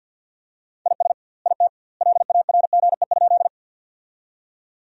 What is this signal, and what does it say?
Morse code, heard as a keyed tone, about 50 wpm, 700 Hz. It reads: IS IT LARGE?
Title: 2x Word-Spacing - Sentences from Top 200 Words - 50wpm